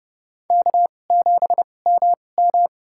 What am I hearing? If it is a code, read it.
Morse code, heard as a keyed tone, about 31 wpm, spaced as for 22 wpm, 700 Hz. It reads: K7MM
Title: USA Call Signs - 4 Characters - 2x Character Spacing - 30wpm